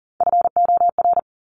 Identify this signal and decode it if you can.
Morse code, heard as a keyed tone, about 38 wpm, 700 Hz. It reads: FOR
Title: Top 200 Words 40wpm